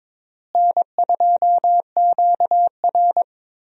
Morse code, heard as a keyed tone, about 22 wpm, 700 Hz. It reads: N2QR
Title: USA Call Signs - 4 Characters - Repeated 3x - 22wpm